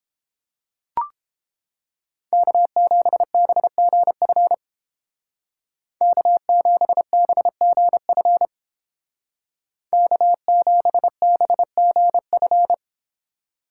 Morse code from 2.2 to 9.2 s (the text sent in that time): K7BGF K7BGF